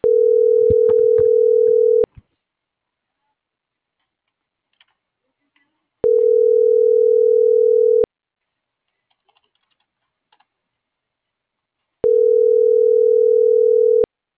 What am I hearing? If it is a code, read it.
Ringback tone (North America), 440 + 480 Hz.